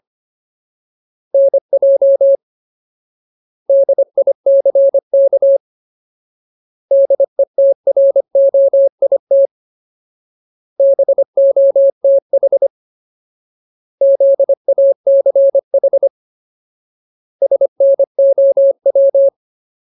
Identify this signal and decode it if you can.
Morse code, heard as a keyed tone, about 25 wpm, 550 Hz. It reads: NJ DICK DETROIT BOTH ZACH SNOW
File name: Ultimate Word List / 25WPM Ultimate